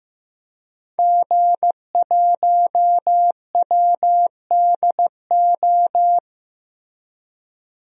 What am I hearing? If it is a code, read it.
Morse code, heard as a keyed tone, about 15 wpm, 700 Hz. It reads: G1WDO
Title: DXCC-100 Call Signs - 5 Characters - Repeated 2x - 15wpm